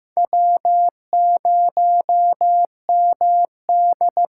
Morse code, heard as a keyed tone, about 15 wpm, 700 Hz. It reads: W0MD
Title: USA Call Signs - 5 Characters - 8x Character Spacing - 15wpm